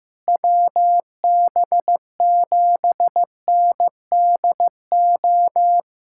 Morse code, heard as a keyed tone, about 16 wpm, 700 Hz. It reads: WB7NDO
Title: USA Callsign DE Callsign 15wpm